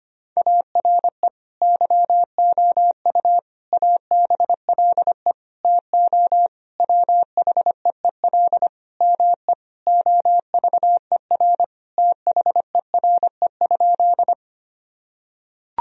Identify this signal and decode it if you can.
Morse code, heard as a keyed tone, about 25 wpm, 700 Hz. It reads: ARE YOU ABLE TO WHEEL ME OVER THERE?
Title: Sentences from Top 1000 Words - Review Part 3 - 25wpm